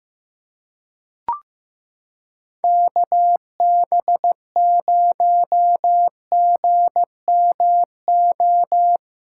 Morse code, heard as a keyed tone, about 15 wpm, 700 Hz. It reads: KB0GMO
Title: USA Callsign DE Callsign 15wpm